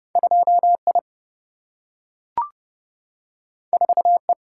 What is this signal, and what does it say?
Morse code, heard as a keyed tone, about 30 wpm, 700 Hz. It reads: 2I 4E
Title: Single-Letter Single-Number 30wpm